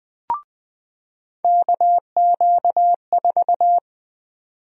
Morse code, heard as a keyed tone, about 20 wpm, 700 Hz. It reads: KQ4
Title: USA Call Sign Prefixes - 20wpm